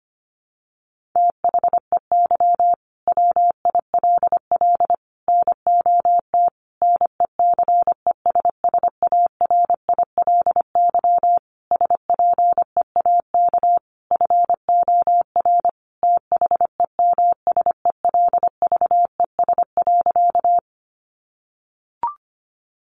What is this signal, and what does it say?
Morse code, heard as a keyed tone, about 25 wpm, 700 Hz. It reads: THEY WILL NOT NECESSARILY SPEAK FOR THEMSELVES.
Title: Sentences from Top 1400 Words 25wpm